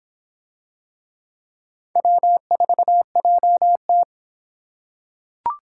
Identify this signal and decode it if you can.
Morse code, heard as a keyed tone, about 26 wpm, 700 Hz. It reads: W4JT